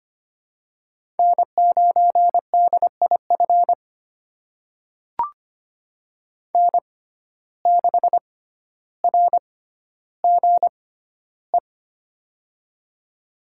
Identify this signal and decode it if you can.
Morse code, heard as a keyed tone, about 25 wpm, spaced as for 13 wpm, 700 Hz. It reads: N9DIF N6RGE
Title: USA Call Signs - 6x Character Spacing - 25wpm